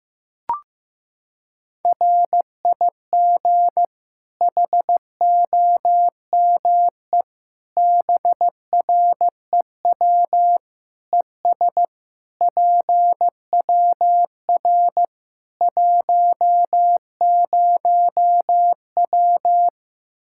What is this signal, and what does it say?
Morse code, heard as a keyed tone, about 15 wpm, 700 Hz. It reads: RIG HOME BREW ES PWR 10W
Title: Practice QSO - Exchange 2 - Info - 15wpm